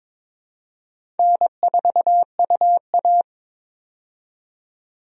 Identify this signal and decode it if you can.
Morse code, heard as a keyed tone, about 22 wpm, 700 Hz. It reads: N4UA
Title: USA Call Signs - 4 Characters - 22wpm